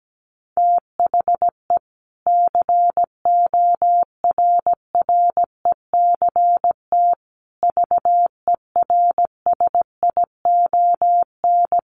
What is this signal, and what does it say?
Morse code, heard as a keyed tone, about 17 wpm, 700 Hz. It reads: THE CORRECT VERSION
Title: Sentences from Top 1400 Words 17wpm